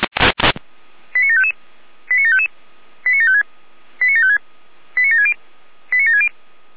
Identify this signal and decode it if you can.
Five-tone selective call (ZVEI), heard as quick runs of five tones, tones of about 70 ms. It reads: 89755 89755 89756 89756 89760 89760
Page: SDR/ZVEI